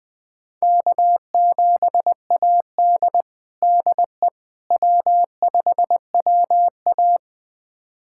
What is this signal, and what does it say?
Morse code, heard as a keyed tone, about 20 wpm, 700 Hz. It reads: K7AD DE W5WA
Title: USA Callsign DE Callsign - 4 Characters - 20wpm